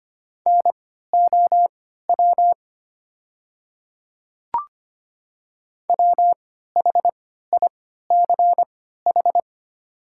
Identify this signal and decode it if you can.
Morse code, heard as a keyed tone, about 25 wpm, spaced as for 14 wpm, 700 Hz. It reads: NOW WHICH
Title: 3x Character Spacing - Top 100 Words - 25wpm